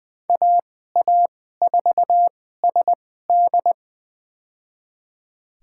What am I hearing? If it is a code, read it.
Morse code, heard as a keyed tone, about 20 wpm, spaced as for 14 wpm, 700 Hz. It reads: AA4SD
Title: USA Call Signs - 5 Characters - 2x Character Spacing - 20wpm